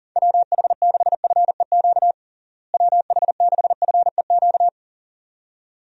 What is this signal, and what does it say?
Morse code, heard as a keyed tone, about 40 wpm, 700 Hz. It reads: WH6FEQ WH6FEQ
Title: USA Call Signs - 6 Characters - Repeated 2x - 40wpm